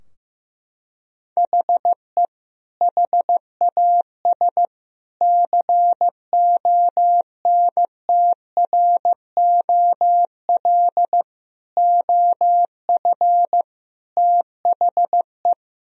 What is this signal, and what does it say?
Morse code, heard as a keyed tone, about 15 wpm, 700 Hz. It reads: HE HAS CONTROL OF THE
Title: Sentences from Top 1000 Words - Review Part 3 - 15wpm